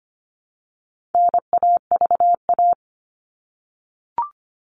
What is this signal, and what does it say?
Morse code, heard as a keyed tone, about 25 wpm, 700 Hz. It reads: NAVA
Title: All Cut Numbers - 4-Digit 25wpm